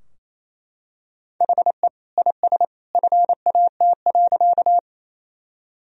Morse code, heard as a keyed tone, about 28 wpm, 700 Hz. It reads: HE IS FAT.
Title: Sentences from Top 1000 Words - Review Part 4 - 28wpm